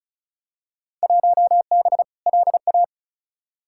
Morse code, heard as a keyed tone, about 36 wpm, 700 Hz. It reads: 1B LA
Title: Field Day Exchanges - 35wpm